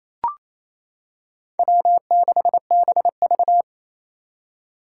Morse code, heard as a keyed tone, about 27 wpm, 700 Hz. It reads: W6BV
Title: Four Challenging Letter-Number Combinations 28wpm